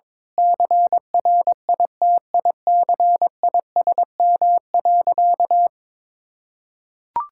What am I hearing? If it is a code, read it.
Morse code, heard as a keyed tone, about 22 wpm, 700 Hz. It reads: CRITICISM.
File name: practice / 22wpm / Sentences with 8-Word Phrases - 22wpm